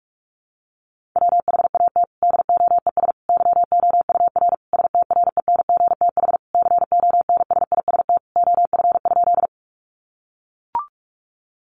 Morse code, heard as a keyed tone, about 44 wpm, 700 Hz. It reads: WHAT DOES YOUR STRENGTH CONSIST OF?
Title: Sentences from Top 1400 Words 45wpm